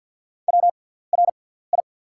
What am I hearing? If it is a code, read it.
Morse code, heard as a keyed tone, about 48 wpm, spaced as for 17 wpm, 690 Hz. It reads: WRI
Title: 6x Character Spacing - Top 100 Words - 50wpm